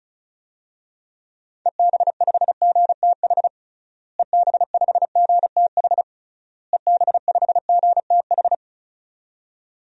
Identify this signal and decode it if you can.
Morse code, heard as a keyed tone, about 36 wpm, 700 Hz. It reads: EB5GTH EB5GTH EB5GTH
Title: DXCC-100 Call Signs - 6 Characters - Repeated 3x - 35wpm